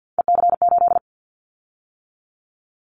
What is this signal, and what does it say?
Morse code, heard as a keyed tone, about 50 wpm, 700 Hz. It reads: EC8
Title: Callsign Prefixes for Top 100 DXCC - 50wpm